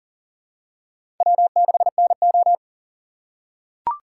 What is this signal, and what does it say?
Morse code, heard as a keyed tone, about 40 wpm, 700 Hz. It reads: W6NO
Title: USA Call Signs - 6x Character Spacing - 40wpm